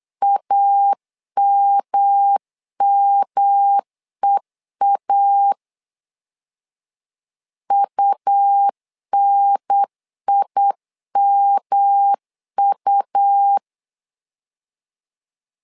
Morse code, heard as a keyed tone, about 8 wpm, 800 Hz. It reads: AMMEA UNIMU
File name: LEZIONI / 3 MWRO / 3 M 8wpm 2300